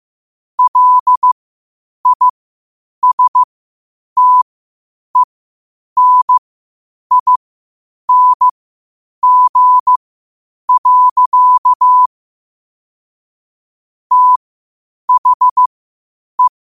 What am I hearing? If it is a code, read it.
Morse code, heard as a keyed tone, about 15 wpm, spaced as for 8 wpm, 1000 Hz. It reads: LISTENING. THE
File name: qotdfiles / 8wpm